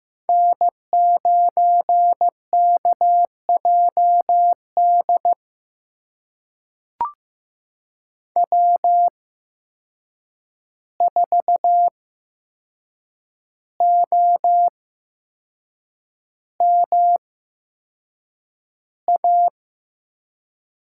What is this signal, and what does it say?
Morse code, heard as a keyed tone, about 15 wpm, spaced as for 6 wpm, 700 Hz. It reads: N9KJD W4OMA
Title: USA Call Signs - 5 Characters - 8x Character Spacing - 15wpm